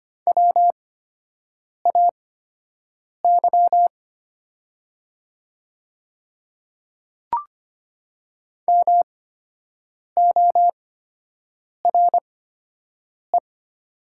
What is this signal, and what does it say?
Morse code, heard as a keyed tone, about 25 wpm, spaced as for 7 wpm, 700 Hz. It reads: WAY MORE
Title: 8x Character Spacing - Top 100 Words - 25wpm